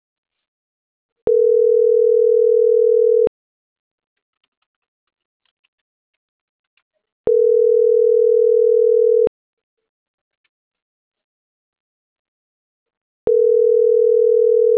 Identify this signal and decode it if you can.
Ringback tone (North America), 440 + 480 Hz.